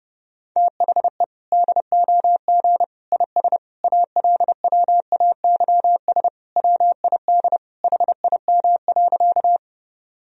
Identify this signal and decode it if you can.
Morse code, heard as a keyed tone, about 28 wpm, 700 Hz. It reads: THE DOG IS ALWAYS WID HIM.
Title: Sentences from Top 400 Words and Abbreviations 30wpm